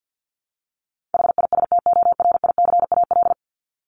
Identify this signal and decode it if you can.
Morse code, heard as a keyed tone, about 50 wpm, 700 Hz. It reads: HISTORICAL
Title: Top 501-750 Words 50wpm